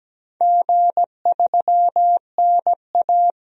Morse code, heard as a keyed tone, about 17 wpm, 700 Hz. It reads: G3NA
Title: DXCC-100 Call Signs - 4 Characters - Repeated 2x - 17wpm